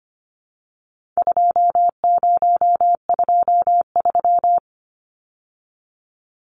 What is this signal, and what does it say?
Morse code, heard as a keyed tone, about 25 wpm, 700 Hz. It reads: 2023
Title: Numbers - 4-Digit 25wpm